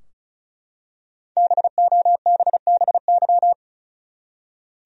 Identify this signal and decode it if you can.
Morse code, heard as a keyed tone, about 35 wpm, 700 Hz. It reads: BOBBY
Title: USA Common Names - Max 5 Letters - 35wpm